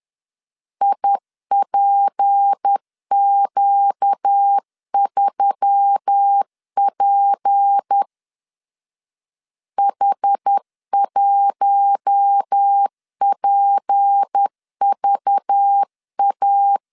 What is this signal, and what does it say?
Morse code, heard as a keyed tone, about 10 wpm, 800 Hz. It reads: IPQ3P H1PVA
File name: LEZIONI / 6 PKX0 / 6 P 10wpm 2432